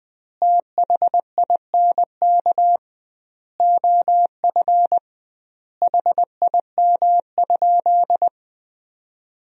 Morse code, heard as a keyed tone, about 21 wpm, 700 Hz. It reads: THINK OF HIM?
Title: 2x Word-Spacing - Sentences from Top 200 Words - 20wpm